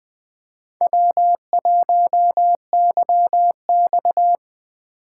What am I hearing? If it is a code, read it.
Morse code, heard as a keyed tone, about 20 wpm, 700 Hz. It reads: W1YX